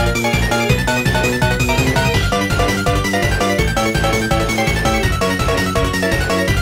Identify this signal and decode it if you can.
There is no code.